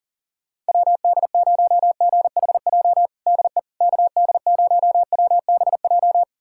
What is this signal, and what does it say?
Morse code, heard as a keyed tone, about 40 wpm, 700 Hz. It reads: WD0GHJ DE KD0WBJ
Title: USA Callsign DE Callsign - 6 Characters - 40wpm